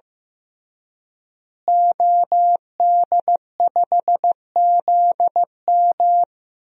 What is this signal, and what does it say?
Morse code, heard as a keyed tone, about 15 wpm, 700 Hz. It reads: OD5ZM
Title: DXCC-100 Call Signs - 5 Characters - Repeated 3x - 15wpm